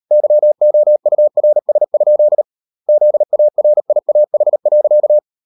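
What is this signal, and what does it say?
Morse code, heard as a keyed tone, about 38 wpm, 600 Hz. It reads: YOURS? ZARIAH.
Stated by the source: Morse code practice at 38 WPM, sending Ken Follett's novel Never in English, from CW med Gnister.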